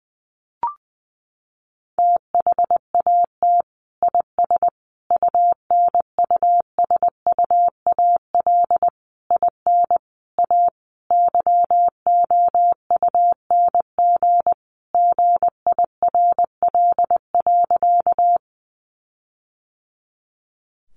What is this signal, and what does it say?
Morse code, heard as a keyed tone, about 20 wpm, 700 Hz. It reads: THAT IS UNUSUAL IN A YOUNG GIRL.
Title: Sentences from Top 1400 Words 20wpm